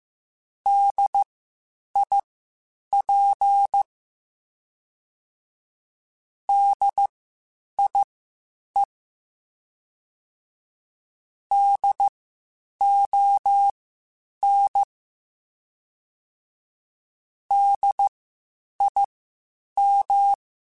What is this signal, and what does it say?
Morse code, heard as a keyed tone, about 15 wpm, spaced as for 8 wpm, 800 Hz. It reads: DIP DIE DON DIM
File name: Spaced Alphas / Spaced Alphas mp3s / Ds sp @13